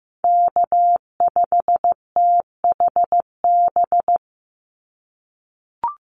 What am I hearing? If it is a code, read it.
Morse code, heard as a keyed tone, about 15 wpm, 700 Hz. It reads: K5THB
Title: USA Call Signs 15wpm